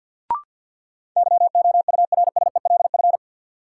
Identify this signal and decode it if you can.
Morse code, heard as a keyed tone, about 52 wpm, 700 Hz. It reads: YOURSELF
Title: Sentences from Top 600 Words - New Word Drill - 50wpm